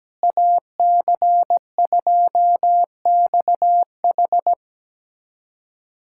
Morse code, heard as a keyed tone, about 17 wpm, 700 Hz. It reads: AC2XH
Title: USA Callsign DE Callsign 17wpm